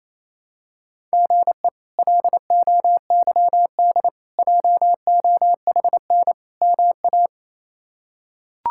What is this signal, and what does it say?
Morse code, heard as a keyed tone, about 28 wpm, 700 Hz. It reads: GE LOYD JOHN MA
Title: K1USN SST QSO Practice - 28wpm